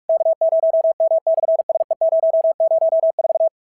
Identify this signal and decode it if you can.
Morse code, heard as a keyed tone, about 46 wpm, 690 Hz. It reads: K0M/SE004
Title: SOTA QSO Practice 45wpm